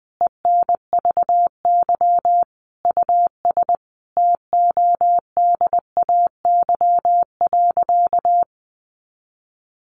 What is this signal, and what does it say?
Morse code, heard as a keyed tone, about 20 wpm, 700 Hz. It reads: ENVY US TODAY.
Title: A Sound Of Thunder 20wpm